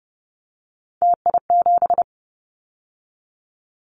Morse code, heard as a keyed tone, about 30 wpm, 700 Hz. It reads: TI7
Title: Callsign Prefixes for Top 100 DXCC - 30wpm